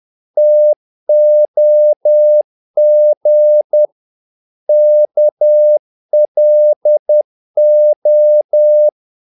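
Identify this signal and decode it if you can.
Morse code, heard as a keyed tone, about 10 wpm, 600 Hz. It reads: TOG KLO
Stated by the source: Morse practice of short Danish words at 10 wpm, from CW med Gnister.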